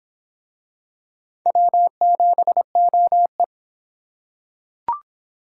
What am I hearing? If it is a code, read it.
Morse code, heard as a keyed tone, about 24 wpm, 700 Hz. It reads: W7OE